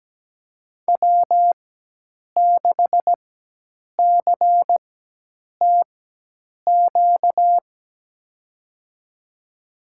Morse code, heard as a keyed tone, about 17 wpm, spaced as for 8 wpm, 700 Hz. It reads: W6CTQ
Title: USA Call Signs - 5 Characters - 4x Character Spacing - 17wpm